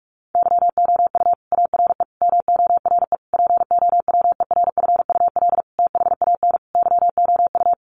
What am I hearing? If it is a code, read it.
Morse code, heard as a keyed tone, about 44 wpm, 700 Hz. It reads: YOU ARE MORE POWERFUL THAN YOU
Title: Sentences from Top 1300 Words 45wpm